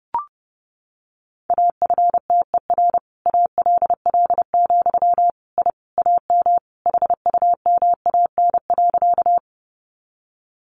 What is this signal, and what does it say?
Morse code, heard as a keyed tone, about 30 wpm, 700 Hz. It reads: AFTER ALL, I AM HUMAN.
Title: Sentences from Top 600 Words 30wpm